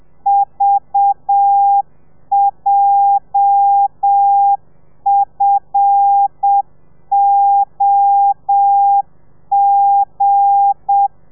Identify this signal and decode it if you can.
Morse code, heard as a keyed tone, about 7 wpm, 800 Hz. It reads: VJFOG
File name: telegrafia / i2tfj / Lez8 30